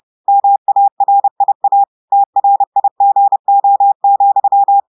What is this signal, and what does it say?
Morse code, heard as a keyed tone, about 30 wpm, 800 Hz. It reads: MARIA TRIGO,
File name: audios / portugais / portugais texte en clair 1 vitesse 30 mots minutes